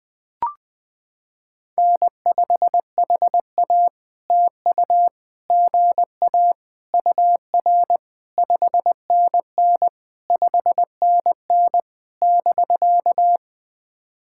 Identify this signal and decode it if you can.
Morse code, heard as a keyed tone, about 20 wpm, 690 Hz. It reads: N5HA TU GA UR 5NN 5NN <BK>
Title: POTA QSO Practice - 20wpm